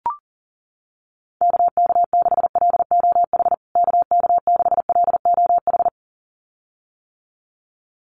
Morse code, heard as a keyed tone, about 40 wpm, 700 Hz. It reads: KK6LOH KK6LOH
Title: USA Call Signs Repeated 2x 40wpm